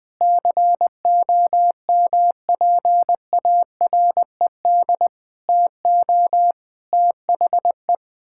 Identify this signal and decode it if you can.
Morse code, heard as a keyed tone, about 21 wpm, 700 Hz. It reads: COMPARED TO THE
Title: Sentences with 7-Word Phrases - 20wpm